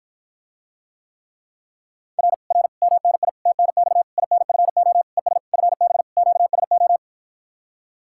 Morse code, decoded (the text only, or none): R R MNI TNX INFO ES FB QSO